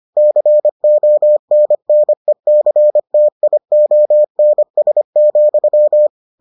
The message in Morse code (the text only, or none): CONNECTIONS,